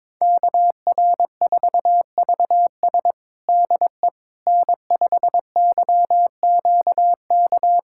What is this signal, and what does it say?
Morse code, heard as a keyed tone, about 21 wpm, 700 Hz. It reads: KR4VS DE N5YQK
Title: USA Callsign DE Callsign - 5 Characters - 22wpm